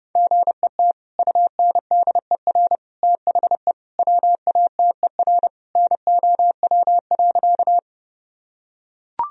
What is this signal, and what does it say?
Morse code, heard as a keyed tone, about 32 wpm, 700 Hz. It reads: GET UNDER THE WATER NOW.
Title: Sentences from Top 1000 Words - Review Part 1 - 30wpm